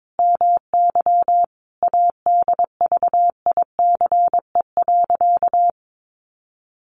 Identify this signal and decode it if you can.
Morse code, heard as a keyed tone, about 22 wpm, 700 Hz. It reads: MY ADVICE.
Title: Sentences from Top 1100 Words 22wpm